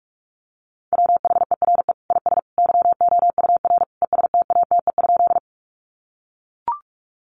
Morse code, heard as a keyed tone, about 44 wpm, 700 Hz. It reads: WHERE IS YOUR ESTATE?
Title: Sentences from Top 1200 Words 45wpm